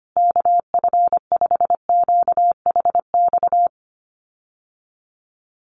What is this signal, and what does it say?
Morse code, heard as a keyed tone, about 25 wpm, 700 Hz. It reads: KF5QHX
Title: USA Call Signs Repeated 2x 25wpm